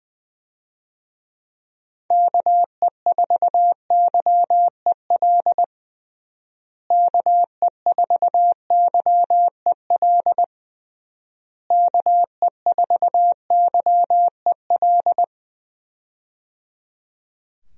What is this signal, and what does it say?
Morse code, heard as a keyed tone, about 21 wpm, 700 Hz. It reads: KE4YEL KE4YEL KE4YEL
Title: USA Call Signs - 6 Characters - Repeated 3x - 20wpm